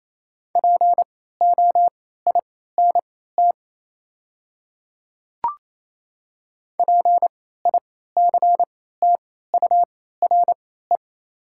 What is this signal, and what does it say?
Morse code, heard as a keyed tone, about 28 wpm, spaced as for 16 wpm, 700 Hz. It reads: POINT PICTURE
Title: 3x Character Spacing - Top 300 Words - 28wpm